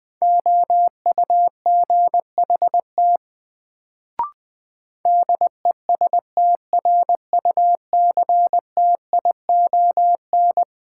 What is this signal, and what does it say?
Morse code, heard as a keyed tone, about 20 wpm, 700 Hz. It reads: OUGHT DESTRUCTION
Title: Sets of 5 Words Encore - New Word Drill - 20wpm